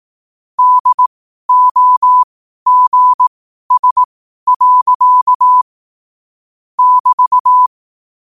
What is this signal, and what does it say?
Morse code, heard as a keyed tone, about 18 wpm, spaced as for 12 wpm, 1000 Hz. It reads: DOGS. <BT>
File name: qotdfiles / 12wpm